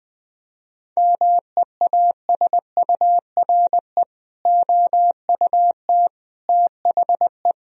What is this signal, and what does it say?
Morse code, heard as a keyed tone, about 21 wpm, 700 Hz. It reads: MEASURE OUT THE
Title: Sentences from Top 1000 Words - Review Part 2 - 20wpm